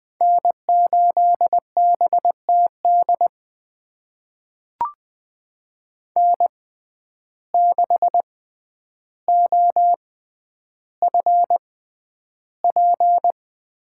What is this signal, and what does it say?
Morse code, heard as a keyed tone, about 20 wpm, spaced as for 10 wpm, 700 Hz. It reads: N8BTD N6OFP
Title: USA Call Signs - 5 Characters - 6x Character Spacing - 20wpm